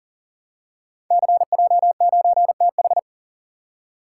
Morse code, heard as a keyed tone, about 39 wpm, 700 Hz. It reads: CJ9TH
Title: Code Groups - 5 Letter:Numbers - 40wpm